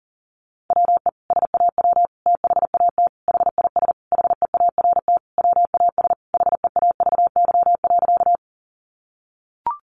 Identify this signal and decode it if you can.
Morse code, heard as a keyed tone, about 40 wpm, 700 Hz. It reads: WE SAW THAT HIS HEART WAS HEAVY.
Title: Sentences from Top 1200 Words 40wpm